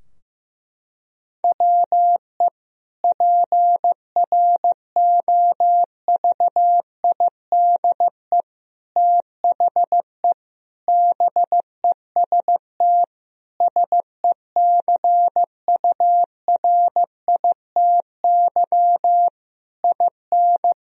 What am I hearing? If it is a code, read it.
Morse code, heard as a keyed tone, about 15 wpm, 700 Hz. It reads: WE PROVIDE THE BEST SECURITY IN
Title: Sentences from Top 1000 Words - Review Part 4 - 15wpm